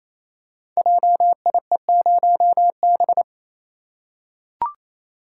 Morse code, heard as a keyed tone, about 27 wpm, 700 Hz. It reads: JIE0B
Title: Code Groups - 5 Letter:Numbers - 28wpm